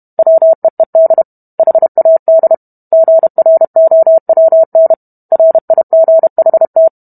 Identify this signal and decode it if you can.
Morse code, heard as a keyed tone, about 32 wpm, 650 Hz. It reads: WEED HAD GROWN RIGHT